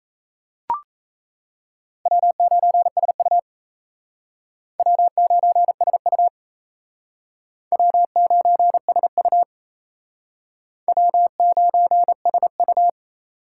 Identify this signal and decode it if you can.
Morse code, heard as a keyed tone, about 36 wpm, 700 Hz. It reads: W9SU W9SU W9SU W9SU